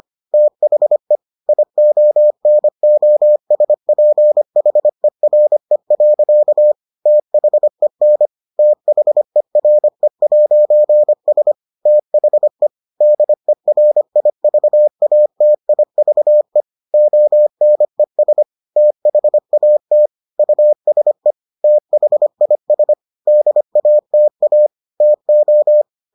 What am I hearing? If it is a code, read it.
Morse code, heard as a keyed tone, about 25 wpm, 600 Hz. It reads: THE IONOSPHERE. THEN THERE'S THE DERIVATIVE ONES THAT USE THIS DATA TO